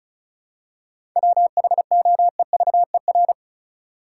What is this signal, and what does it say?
Morse code, heard as a keyed tone, about 36 wpm, 700 Hz. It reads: WHOEVER
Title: Seven Letter Words - 35wpm